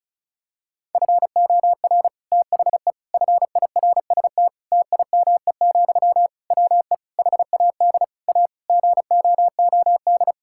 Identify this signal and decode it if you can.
Morse code, heard as a keyed tone, about 36 wpm, 700 Hz. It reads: FOR THE FIRST TIME, WE HAD A GOOD
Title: Sentences from Top 1000 Words - Review Part 6 - 35wpm